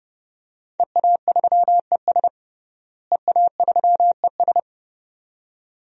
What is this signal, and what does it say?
Morse code, heard as a keyed tone, about 32 wpm, 700 Hz. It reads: EA3ES EA3ES
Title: DXCC-100 Call Signs - 5 Characters - Repeated 2x - 30wpm